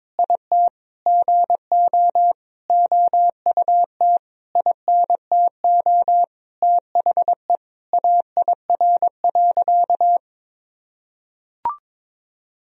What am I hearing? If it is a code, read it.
Morse code, heard as a keyed tone, about 22 wpm, 700 Hz. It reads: IT GO OUT INTO THE AIR.
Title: Sentences from Top 1000 Words - Review Part 1 - 22wpm